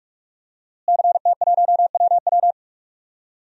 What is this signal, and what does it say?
Morse code, heard as a keyed tone, about 42 wpm, 700 Hz. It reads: CT1WW
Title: DXCC-100 Call Signs - 5 Characters - Repeated 3x - 45wpm